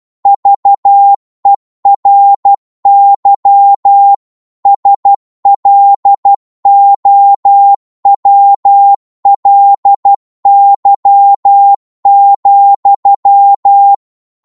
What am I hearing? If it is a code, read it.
Morse code, heard as a keyed tone, about 12 wpm, 800 Hz. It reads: VERY SLOWLY,
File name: cwmp3 / psg-1-12wpm